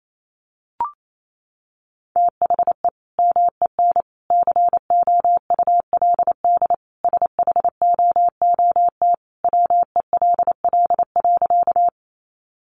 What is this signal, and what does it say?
Morse code, heard as a keyed tone, about 28 wpm, 700 Hz. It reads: THE MEN COULD SHOOT WELL.
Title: Sentences from Top 1200 Words 28wpm